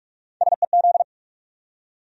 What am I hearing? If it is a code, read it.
Morse code, heard as a keyed tone, about 40 wpm, 700 Hz. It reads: SEZ